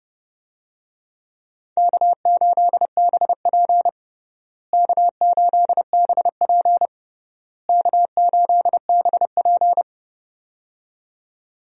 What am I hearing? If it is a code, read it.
Morse code, heard as a keyed tone, about 30 wpm, 700 Hz. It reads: K8BP K8BP K8BP
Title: USA Call Signs - 4 Characters - Repeated 3x - 30wpm